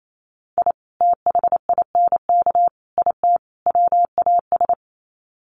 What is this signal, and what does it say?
Morse code, heard as a keyed tone, about 28 wpm, 700 Hz. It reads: I THINK IT WAS
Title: Sets of 4 Words 28wpm